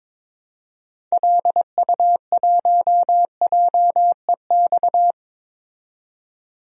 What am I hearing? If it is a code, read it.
Morse code, heard as a keyed tone, about 22 wpm, 700 Hz. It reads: LU1JEX
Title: DXCC-100 Call Signs - 6 Characters - 22wpm